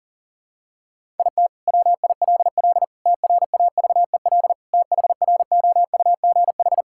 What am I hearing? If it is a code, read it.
Morse code, heard as a keyed tone, about 40 wpm, 700 Hz. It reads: IT WILL TRAVEL THROUGH
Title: Sentences from Top 1000 Words - Review Part 6 - 40wpm